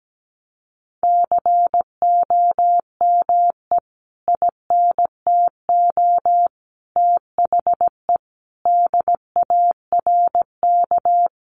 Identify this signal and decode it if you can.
Morse code, heard as a keyed tone, about 17 wpm, 700 Hz. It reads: COME INTO THE DARK
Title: Sentences from Top 1000 Words 17wpm